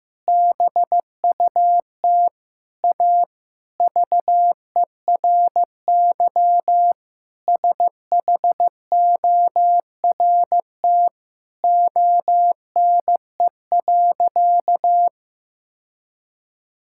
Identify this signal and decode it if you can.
Morse code, heard as a keyed tone, about 15 wpm, 700 Hz. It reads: BUT A VERY SHORT ONE.
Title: Sentences from Top 1600 Words 15wpm